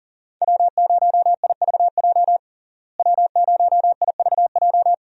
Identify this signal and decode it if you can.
Morse code, heard as a keyed tone, about 40 wpm, 700 Hz. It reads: W0IVJ W0IVJ
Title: USA Call Signs - 5 Characters - Repeated 2x - 40wpm